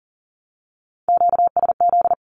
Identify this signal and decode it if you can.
Morse code, heard as a keyed tone, about 40 wpm, 700 Hz. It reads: QSZ
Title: Q-Codes 40wpm